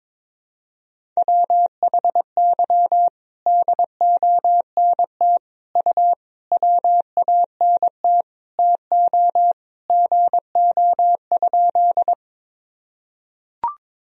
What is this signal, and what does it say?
Morse code, heard as a keyed tone, about 22 wpm, 700 Hz. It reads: WHY DONT U WANT TO GO?
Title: Sentences from Top 300 Words and Abbreviations 22wpm